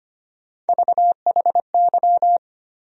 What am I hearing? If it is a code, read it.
Morse code, heard as a keyed tone, about 25 wpm, 700 Hz. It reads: VHY
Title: USA Call Sign Postfixes - 3 Letters 25wpm